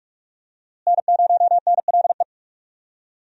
Morse code, heard as a keyed tone, about 44 wpm, 700 Hz. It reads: N0NLE
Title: USA Call Signs - 5 Characters - Repeated 3x - 45wpm